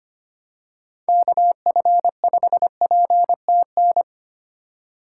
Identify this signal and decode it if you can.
Morse code, heard as a keyed tone, about 25 wpm, 700 Hz. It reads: KF5PTN
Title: USA Call Signs - 6 Characters - 3x Character Spacing - 25wpm